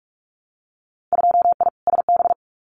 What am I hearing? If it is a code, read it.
Morse code, heard as a keyed tone, about 46 wpm, 700 Hz. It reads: 2I SB
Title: Winter Field Day Exchanges Encore 45wpm